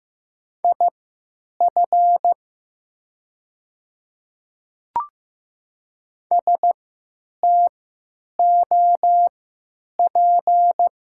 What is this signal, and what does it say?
Morse code, heard as a keyed tone, about 15 wpm, spaced as for 9 wpm, 700 Hz. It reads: IF STOP
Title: 3x Character Spacing - Top 500 Words - 15wpm